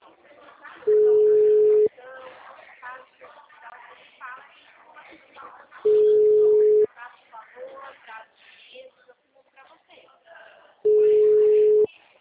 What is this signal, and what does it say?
Ringback tone (Europe, 425 Hz), with speech.